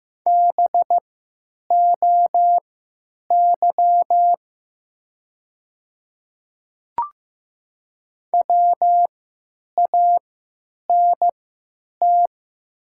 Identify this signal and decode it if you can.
Morse code, heard as a keyed tone, about 15 wpm, spaced as for 9 wpm, 700 Hz. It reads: BOY WANT
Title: 3x Character Spacing - Top 200 Words - 15wpm